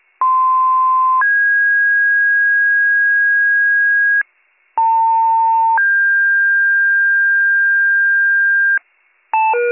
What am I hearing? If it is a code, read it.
Two-tone page, heard as a one-second tone, then a three-second tone, about 1040 then 1650 Hz, about 900 then 1600 Hz. It is heard from 0.2 to 8.8 s.